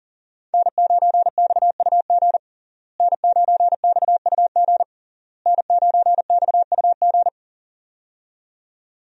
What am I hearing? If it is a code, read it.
Morse code, heard as a keyed tone, about 40 wpm, 700 Hz. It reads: N9XUG N9XUG N9XUG
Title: USA Call Signs - 5 Characters - Repeated 3x - 40wpm